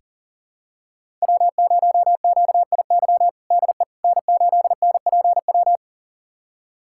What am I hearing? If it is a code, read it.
Morse code, heard as a keyed tone, about 40 wpm, 700 Hz. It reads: W0QIY DE N8NPW
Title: USA Callsign DE Callsign - 5 Characters - 40wpm